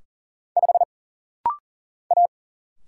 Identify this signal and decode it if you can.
Morse code, heard as a keyed tone, about 40 wpm, 700 Hz. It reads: 5 A
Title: Single Letter-Number Alt - Rapid-Fire 40wpm